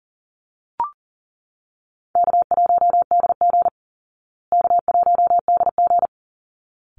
Morse code, heard as a keyed tone, about 40 wpm, 700 Hz. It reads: K1DG K1DG
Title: Easy Call Signs Repeated 2x - 40wpm